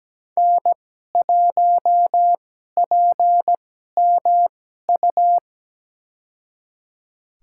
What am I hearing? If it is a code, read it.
Morse code, heard as a keyed tone, about 17 wpm, spaced as for 12 wpm, 700 Hz. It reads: N1PMU
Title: USA Call Signs - 2x Character Spacing - 17wpm